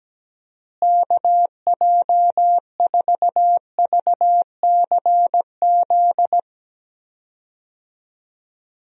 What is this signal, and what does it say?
Morse code, heard as a keyed tone, about 17 wpm, 700 Hz. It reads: KJ4VCZ
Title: USA Call Signs - 6 Characters - Repeated 2x - 17wpm